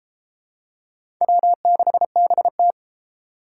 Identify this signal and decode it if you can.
Morse code, heard as a keyed tone, about 30 wpm, 700 Hz. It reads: W6BT